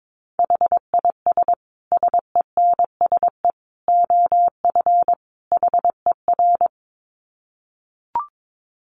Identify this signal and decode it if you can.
Morse code, heard as a keyed tone, about 22 wpm, 700 Hz. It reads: HIS SENSE OF HER
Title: Dit Challenge 22wpm